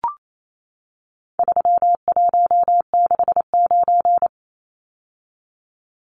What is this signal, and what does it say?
Morse code, heard as a keyed tone, about 28 wpm, 700 Hz. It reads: 3169
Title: Numbers - 4-Digit 28wpm